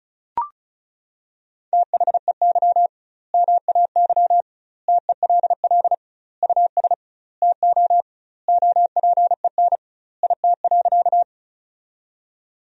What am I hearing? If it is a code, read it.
Morse code, heard as a keyed tone, about 34 wpm, 700 Hz. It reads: THEY MAY TELL US TO OPEN IT.
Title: 2x Word-Spacing - Sentences from Top 300 Words - 35wpm